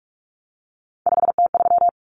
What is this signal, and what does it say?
Morse code, heard as a keyed tone, about 44 wpm, 700 Hz. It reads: 5T3
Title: Common Cut Numbers - 3-Digit 45wpm